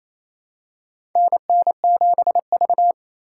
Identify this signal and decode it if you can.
Morse code, heard as a keyed tone, about 27 wpm, 700 Hz. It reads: NN7V
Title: USA Call Signs - 4 Characters - 3x Character Spacing - 28wpm